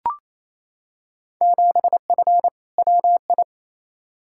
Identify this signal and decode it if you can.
Morse code, heard as a keyed tone, about 28 wpm, 700 Hz. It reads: 7F WI
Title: Field Day Exchanges Encore - 28wpm